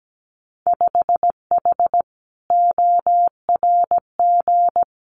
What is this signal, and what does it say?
Morse code, heard as a keyed tone, about 17 wpm, 700 Hz. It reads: 5H ORG
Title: Winter Field Day Exchanges Encore 17wpm